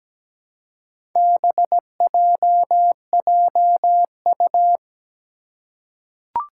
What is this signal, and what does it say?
Morse code, heard as a keyed tone, about 17 wpm, 700 Hz. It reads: BJJU
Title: Four Challenging Letter-Number Combinations 17wpm